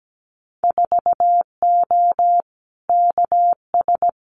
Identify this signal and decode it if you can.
Morse code, heard as a keyed tone, about 17 wpm, 700 Hz. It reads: 4O KS
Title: Winter Field Day Exchanges Encore 17wpm